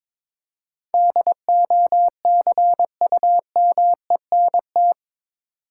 Morse code, heard as a keyed tone, about 22 wpm, 700 Hz. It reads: DOCUMENT
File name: practice / 22wpm / Eight Letter Words - 22wpm